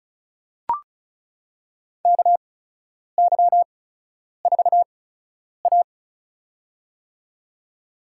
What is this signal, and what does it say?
Morse code, heard as a keyed tone, about 36 wpm, spaced as for 10 wpm, 700 Hz. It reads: KY4A
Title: USA Call Signs - 4 Characters - 8x Character Spacing - 35wpm